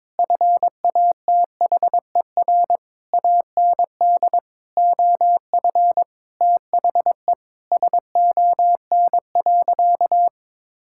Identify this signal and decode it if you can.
Morse code, heard as a keyed tone, about 22 wpm, 700 Hz. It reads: FATHER AND OF THE SON.
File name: Sentences with 9-Word Phrases - 22wpm